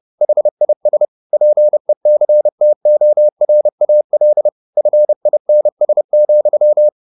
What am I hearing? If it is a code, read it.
Morse code, heard as a keyed tone, about 30 wpm, 600 Hz. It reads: HIS PECTORAL FINS,
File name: the old man and the sea / the old man and the sea 1 30wpm